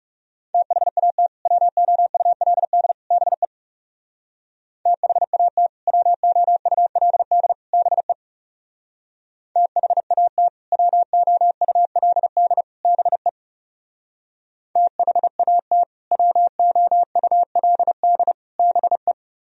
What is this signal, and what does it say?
Morse code, heard as a keyed tone, about 40 wpm, 700 Hz. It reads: THAT WOULD BE THAT WOULD BE THAT WOULD BE THAT WOULD BE